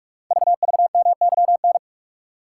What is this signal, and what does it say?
Morse code, heard as a keyed tone, about 44 wpm, 700 Hz. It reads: VVMYN
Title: Code Groups - 5 Letters - 45wpm